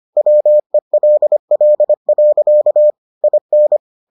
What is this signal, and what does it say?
Morse code, heard as a keyed tone, about 25 wpm, 600 Hz. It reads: WELL. IN